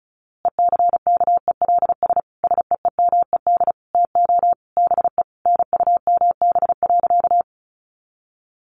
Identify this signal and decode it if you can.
Morse code, heard as a keyed tone, about 34 wpm, 700 Hz. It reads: ECKELS SEEMED TO BE NUMB.
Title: A Sound Of Thunder 35wpm